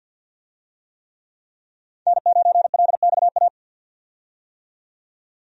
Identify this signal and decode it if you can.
Morse code, heard as a keyed tone, about 50 wpm, 700 Hz. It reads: N9LXA
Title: USA Call Signs - 5 Characters - 50wpm